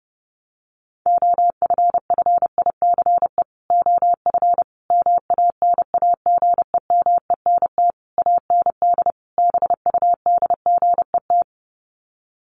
Morse code, heard as a keyed tone, about 30 wpm, 700 Hz. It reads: OFFICE OF MANAGEMENT AND BUDGET
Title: Sets of 5 Words Encore2 30wpm